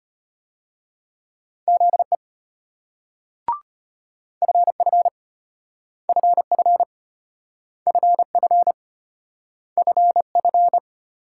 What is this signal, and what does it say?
Morse code, heard as a keyed tone, about 32 wpm, 700 Hz. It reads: ZE FF FF FF FF